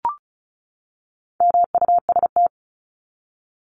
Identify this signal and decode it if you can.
Morse code, heard as a keyed tone, about 34 wpm, 700 Hz. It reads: MUST
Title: Top 500 Words 35wpm